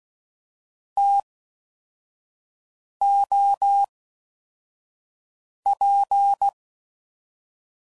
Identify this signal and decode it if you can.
Morse code, heard as a keyed tone, about 16 wpm, spaced as for 4 wpm, 800 Hz. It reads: TOP
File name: Mixed / Variable Sp / TOs